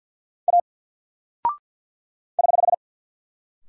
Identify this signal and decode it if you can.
Morse code, heard as a keyed tone, about 48 wpm, 700 Hz. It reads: A <HH>
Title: Single Character (Letter:Number:Punctuation:Prosign) 50wpm